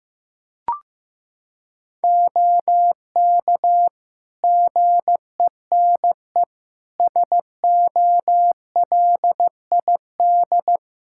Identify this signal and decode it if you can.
Morse code, heard as a keyed tone, about 16 wpm, 700 Hz. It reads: OK GENE SOLID
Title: Practice QSO - Exchange 3 - More Info - 15wpm